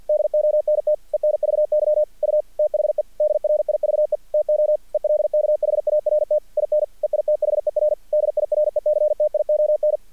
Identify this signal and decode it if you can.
Morse code, heard as a keyed tone, about 48 wpm, 600 Hz. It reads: DONT ENVY U THE DRIVE TO ELKHART IN EITHER DIRECTION
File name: archived / CRSnet